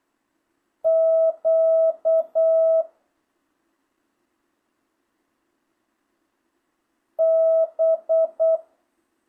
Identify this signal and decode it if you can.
Morse code, heard as a keyed tone, about 8 wpm, 650 Hz. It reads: Q B
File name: jh8cbh / 2025cw